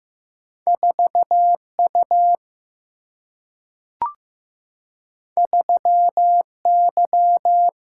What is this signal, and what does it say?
Morse code, heard as a keyed tone, about 15 wpm, 700 Hz. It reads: 4U 3Y
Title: Single-Letter Single-Number 15wpm